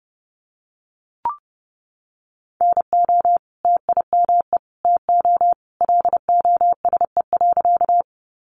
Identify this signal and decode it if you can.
Morse code, heard as a keyed tone, about 30 wpm, 700 Hz. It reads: NO TIME TO LOSE.
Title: Sentences from Top 1100 Words 30wpm